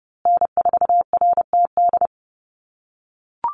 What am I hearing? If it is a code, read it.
Morse code, heard as a keyed tone, about 30 wpm, 700 Hz. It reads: N4RTD
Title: USA Call Signs Repeated 3x 30wpm